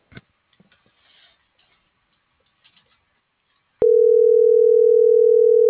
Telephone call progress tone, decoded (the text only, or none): ringback tone